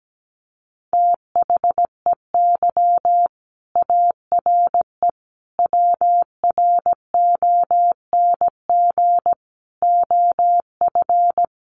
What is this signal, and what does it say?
Morse code, heard as a keyed tone, about 17 wpm, 700 Hz. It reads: THEY ARE WRONG OF
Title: Sentences from Top 900 Words 17wpm